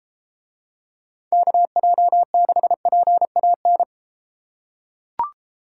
Morse code, heard as a keyed tone, about 31 wpm, 700 Hz. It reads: KJ6PAN